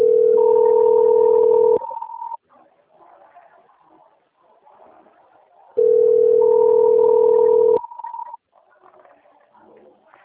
Ringback tone (North America), 440 + 480 Hz.
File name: out- - - - - .10346